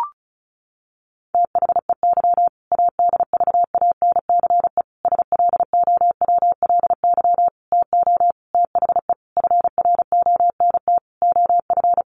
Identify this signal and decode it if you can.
Morse code, heard as a keyed tone, about 34 wpm, 700 Hz. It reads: THEY ADVANCE SLOWLY TO THE FRONT OF
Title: Sentences from Top 1400 Words 35wpm